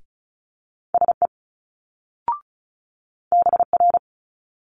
Morse code, heard as a keyed tone, about 33 wpm, 700 Hz. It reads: SE BR
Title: Multi-letter Phonemes Quick Fire 35wpm